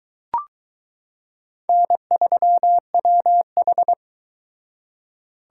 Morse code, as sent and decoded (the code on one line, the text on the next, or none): -. ...-- .-- ....
N3WH